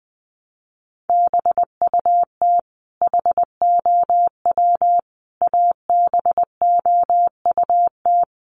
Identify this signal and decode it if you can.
Morse code, heard as a keyed tone, about 20 wpm, 700 Hz. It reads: BUT HOW ABOUT
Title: Sentences from Top 100 Words 20wpm